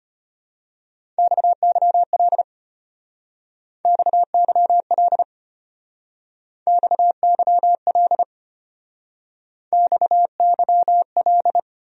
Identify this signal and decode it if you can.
Morse code, heard as a keyed tone, about 32 wpm, 700 Hz. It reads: XYL XYL XYL XYL